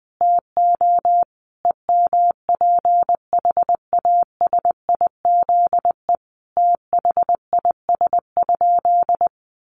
Morse code, heard as a keyed tone, about 20 wpm, 700 Hz. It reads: TO EMPHASIZE THIS?
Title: Sentences from Top 1500 Words 20wpm